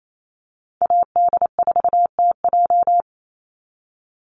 Morse code, heard as a keyed tone, about 28 wpm, 700 Hz. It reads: AD4TJ
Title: Call Signs Repeated 3x - 28wpm